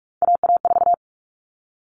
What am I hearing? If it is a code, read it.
Morse code, heard as a keyed tone, about 44 wpm, 700 Hz. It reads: AA4
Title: All Cut Numbers - 3-Digit 45wpm